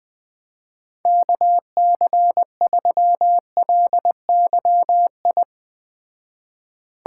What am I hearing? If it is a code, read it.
Morse code, heard as a keyed tone, about 20 wpm, 700 Hz. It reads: KC3LYI
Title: Speed Racing - USA 2x3 Call Signs 17wpm